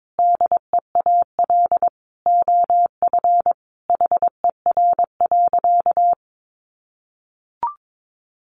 Morse code, heard as a keyed tone, about 22 wpm, 700 Hz. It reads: DEAL OF HER.
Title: Sentences from Top 800 Words 22wpm